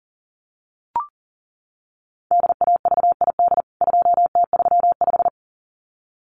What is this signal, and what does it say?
Morse code, heard as a keyed tone, about 40 wpm, 700 Hz. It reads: DAVID 2T35
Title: CWT Exchange 40wpm